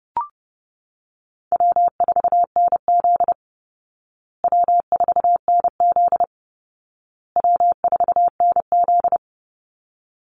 Morse code, heard as a keyed tone, about 30 wpm, 700 Hz. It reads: W4NZ W4NZ W4NZ
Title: Call Signs Repeated 2x - 30wpm